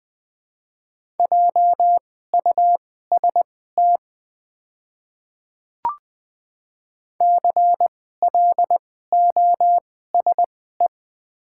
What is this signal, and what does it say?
Morse code, heard as a keyed tone, about 20 wpm, spaced as for 14 wpm, 700 Hz. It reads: JUST CLOSE
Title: 2x Character Spacing - Top 500 Words - 20wpm